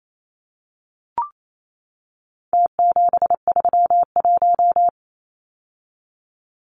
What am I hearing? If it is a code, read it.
Morse code, heard as a keyed tone, about 28 wpm, 700 Hz. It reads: T731
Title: Numbers - Contest Style - 4-Digit 28wpm